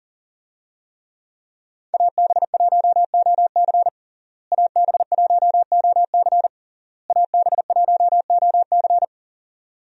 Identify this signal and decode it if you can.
Morse code, heard as a keyed tone, about 40 wpm, 700 Hz. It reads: AB1OC AB1OC AB1OC
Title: USA Call Signs - 5 Characters - Repeated 3x - 40wpm